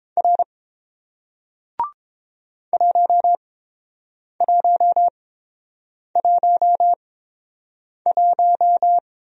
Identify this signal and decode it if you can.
Morse code, heard as a keyed tone, about 29 wpm, spaced as for 7 wpm, 700 Hz. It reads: R 1111